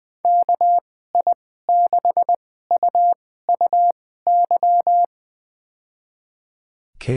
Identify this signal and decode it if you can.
Morse code, heard as a keyed tone, about 21 wpm, spaced as for 15 wpm, 700 Hz. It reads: KI6UUY
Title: USA Call Signs - 6 Characters - 2x Character Spacing - 20wpm